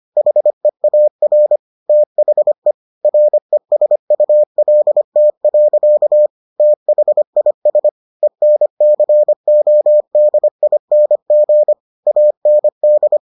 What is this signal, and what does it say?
Morse code, heard as a keyed tone, about 25 wpm, 600 Hz. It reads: HEAR THE RESULT. THIS ENCODING AND